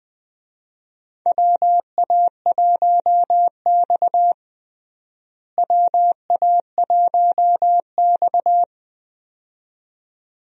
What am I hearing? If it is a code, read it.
Morse code, heard as a keyed tone, about 20 wpm, 700 Hz. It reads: WA1X WA1X
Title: USA Call Signs - 4 Characters - Repeated 2x - 20wpm